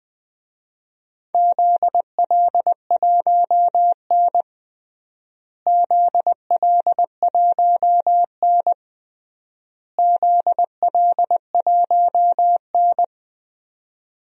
Morse code, heard as a keyed tone, about 20 wpm, 700 Hz. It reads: ZL1N ZL1N ZL1N
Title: DXCC-100 Call Signs - 4 Characters - Repeated 3x - 20wpm